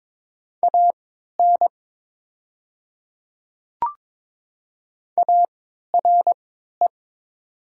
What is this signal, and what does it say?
Morse code, heard as a keyed tone, about 22 wpm, spaced as for 12 wpm, 700 Hz. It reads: AN ARE
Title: 3x Character Spacing - Top 100 Words - 22wpm